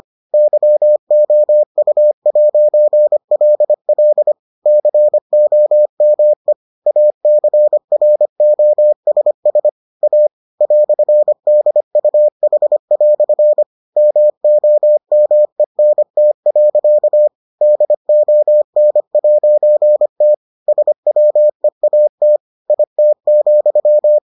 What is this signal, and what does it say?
Morse code, heard as a keyed tone, about 25 wpm, 600 Hz. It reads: YOU'LL COME ACROSS A "DUH" MOMENT. DON'T SWEAT IT,